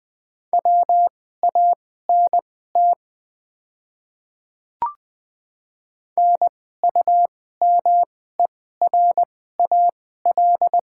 Morse code, heard as a keyed tone, about 20 wpm, spaced as for 15 wpm, 690 Hz. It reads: WANT NUMERAL
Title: 2x Character Spacing - Top 500 Words - 20wpm